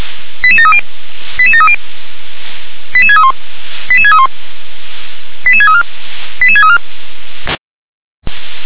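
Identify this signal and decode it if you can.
Five-tone selective call (ZVEI), heard as quick runs of five tones, tones of about 70 ms. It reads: 88730 88730 88631 88631 88634 88634
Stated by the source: ZVEI five-tone call received by SDR.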